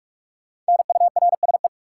Morse code, heard as a keyed tone, about 46 wpm, 700 Hz. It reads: NURSE
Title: Five Letter Words - 45wpm